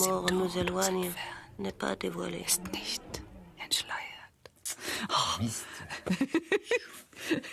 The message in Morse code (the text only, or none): none